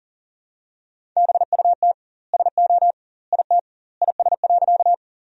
Morse code, heard as a keyed tone, about 40 wpm, 700 Hz. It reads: BUT SO IT IS.
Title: 2x Word-Spacing - Sentences from Top 100 Words - 40wpm